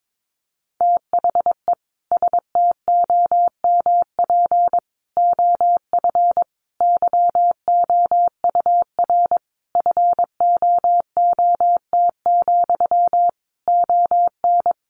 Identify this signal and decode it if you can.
Morse code, heard as a keyed tone, about 22 wpm, 700 Hz. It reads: THE STOMP OF YOUR FOOT, ON
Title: A Sound Of Thunder 22wpm